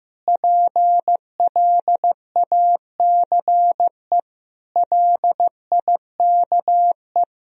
Morse code, heard as a keyed tone, about 15 wpm, 700 Hz. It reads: PLACE LIKE
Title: Sentences from Top 200 Words and Abbreviations 15wpm